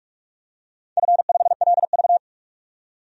Morse code, heard as a keyed tone, about 47 wpm, 700 Hz. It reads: F5LV
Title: DXCC-100 Call Signs - 4 Characters - 45wpm